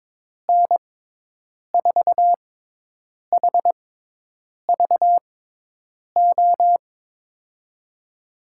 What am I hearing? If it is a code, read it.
Morse code, heard as a keyed tone, about 22 wpm, spaced as for 8 wpm, 700 Hz. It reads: N4HVO
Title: USA Call Signs - 6 Characters - 6x Character Spacing - 22wpm